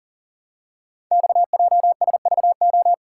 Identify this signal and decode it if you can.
Morse code, heard as a keyed tone, about 40 wpm, 700 Hz. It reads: XJSVO
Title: Code Groups - 5 Letters - 40wpm